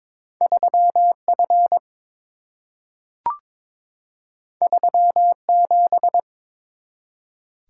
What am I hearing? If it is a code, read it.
Morse code, heard as a keyed tone, about 22 wpm, 700 Hz. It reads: 3F 37
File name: practice / 22wpm / Two Letter-Number Combinations - 22wpm